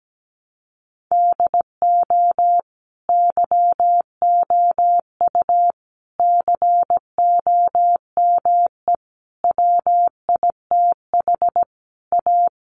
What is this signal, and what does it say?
Morse code, heard as a keyed tone, about 17 wpm, 700 Hz. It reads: DO YOU COME WITH A
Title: Sentences from Top 1200 Words 17wpm